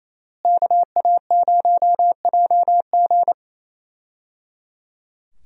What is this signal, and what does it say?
Morse code, heard as a keyed tone, about 28 wpm, 700 Hz. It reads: KA0JG
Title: USA Call Signs - 5 Characters - Repeated 3x - 28wpm